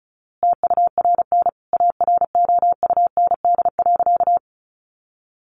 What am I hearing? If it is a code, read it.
Morse code, heard as a keyed tone, about 34 wpm, 700 Hz. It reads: TURN AROUND.
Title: A Sound Of Thunder 35wpm